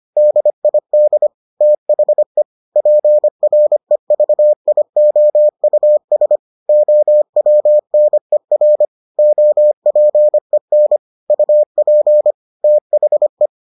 Morse code, heard as a keyed tone, about 25 wpm, 600 Hz. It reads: DID THE PREVIOUS OWNER OPEN UP THE